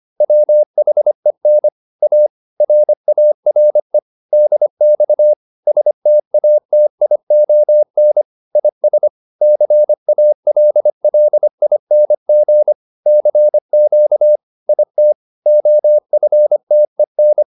Morse code, heard as a keyed tone, about 25 wpm, 600 Hz. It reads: WHEN A RARE DX STATION IS CALLING CQ IT OFTEN